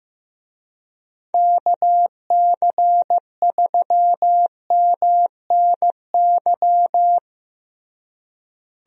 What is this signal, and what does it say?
Morse code, heard as a keyed tone, about 15 wpm, 700 Hz. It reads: KC3MNY
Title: USA Call Signs - 6 Characters - Repeated 2x - 15wpm